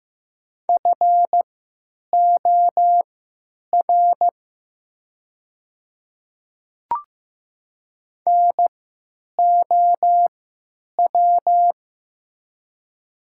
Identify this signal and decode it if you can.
Morse code, heard as a keyed tone, about 15 wpm, spaced as for 9 wpm, 690 Hz. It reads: FOR NOW
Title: 3x Character Spacing - Top 100 Words - 15wpm